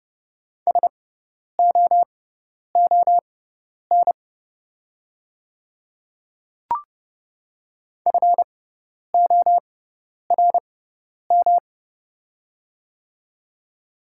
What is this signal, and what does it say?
Morse code, heard as a keyed tone, about 30 wpm, spaced as for 10 wpm, 700 Hz. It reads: SOON FORM
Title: 6x Character Spacing - Top 500 Words - 30wpm